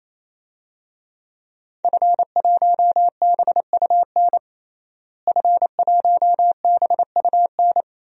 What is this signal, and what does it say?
Morse code, heard as a keyed tone, about 28 wpm, 700 Hz. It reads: F1BUN F1BUN
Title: DXCC-100 Call Signs - 5 Characters - Repeated 3x - 28wpm